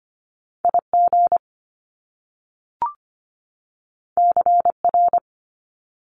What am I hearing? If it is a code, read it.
Morse code, heard as a keyed tone, about 25 wpm, 700 Hz. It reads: IG CR
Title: Top Two Letter Combinations in Words 25wpm